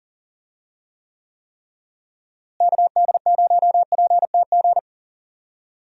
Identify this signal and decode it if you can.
Morse code, heard as a keyed tone, about 40 wpm, 700 Hz. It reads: KD0PTG
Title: USA Call Signs - 6 Characters - Repeated 3x - 40wpm